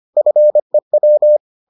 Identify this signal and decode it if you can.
Morse code, heard as a keyed tone, about 25 wpm, 600 Hz. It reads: FEW